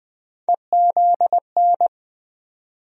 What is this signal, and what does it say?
Morse code, heard as a keyed tone, about 20 wpm, 700 Hz. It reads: EZN
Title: USA Call Sign Postfixes - 3 Letters 20wpm